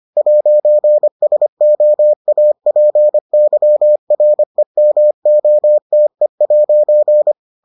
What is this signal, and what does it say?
Morse code, heard as a keyed tone, about 25 wpm, 600 Hz. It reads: 'SOAPYREMOTE'